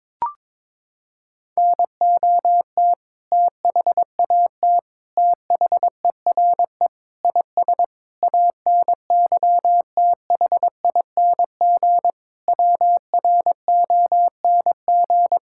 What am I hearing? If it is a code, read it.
Morse code, heard as a keyed tone, about 22 wpm, 700 Hz. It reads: NOT THAT THERE IS ANYTHING WRONG
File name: Sentences with 7-Word Phrases - 22wpm